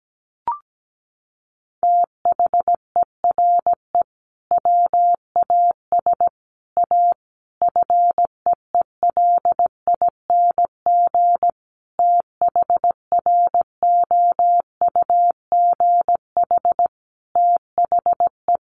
Morse code, heard as a keyed tone, about 17 wpm, 700 Hz. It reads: THERE WAS A FEELING THROUGH THE
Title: Sentences from Top 1200 Words 17wpm